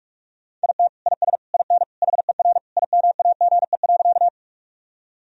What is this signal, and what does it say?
Morse code, heard as a keyed tone, about 43 wpm, 700 Hz. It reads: IT IS IN HER IMAGE.
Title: Sentences from Top 1000 Words - Review Part 4 - 45wpm